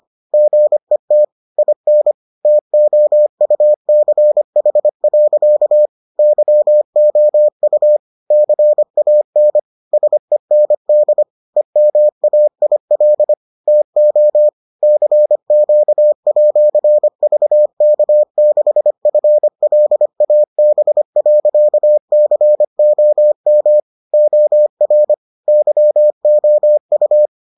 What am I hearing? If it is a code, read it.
Morse code, heard as a keyed tone, about 25 wpm, 600 Hz. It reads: GET IN TOUCH. YOU CAN SEND EMAIL TO CQ@VK6FLAB.COM OR YOU